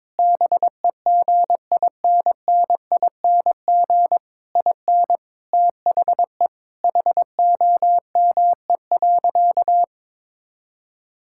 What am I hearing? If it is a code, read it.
Morse code, heard as a keyed tone, about 22 wpm, 700 Hz. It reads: BEGINNING IN THE HOME.
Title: Sentences from Top 1000 Words - Review Part 6 - 22wpm